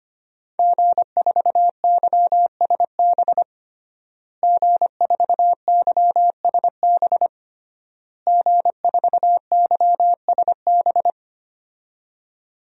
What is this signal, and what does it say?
Morse code, heard as a keyed tone, about 25 wpm, 700 Hz. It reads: G4YSB G4YSB G4YSB
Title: DXCC-100 Call Signs - 5 Characters - Repeated 3x - 25wpm